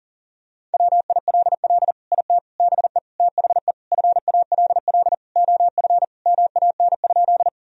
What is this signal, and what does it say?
Morse code, heard as a keyed tone, about 40 wpm, 700 Hz. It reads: WILL IT BE THE FALL OF MAN?
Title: Sentences from Top 1000 Words - Review Part 2 - 40wpm